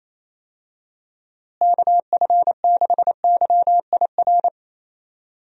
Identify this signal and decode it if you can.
Morse code, heard as a keyed tone, about 28 wpm, 700 Hz. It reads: KF6YIR
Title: USA Call Signs - 6 Characters - Repeated 2x - 28wpm